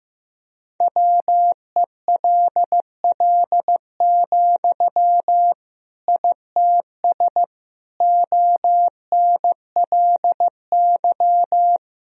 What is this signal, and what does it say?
Morse code, heard as a keyed tone, about 15 wpm, 700 Hz. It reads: WELL, ITS ONLY
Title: Sentences from Top 1000 Words - Review Part 3 - 15wpm